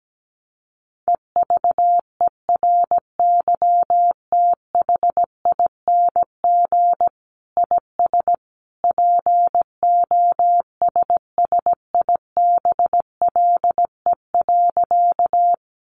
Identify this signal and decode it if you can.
Morse code, heard as a keyed tone, about 17 wpm, 700 Hz. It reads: EVERYTHING IS POSSIBLE.
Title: Sentences from Top 700 Words 17wpm